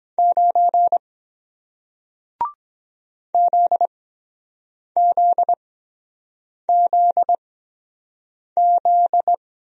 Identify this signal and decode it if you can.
Morse code, heard as a keyed tone, about 23 wpm, spaced as for 7 wpm, 700 Hz. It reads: 9 ZZZZ